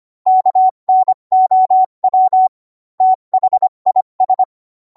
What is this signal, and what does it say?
Morse code, heard as a keyed tone, about 25 wpm, 750 Hz. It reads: KNOW THIS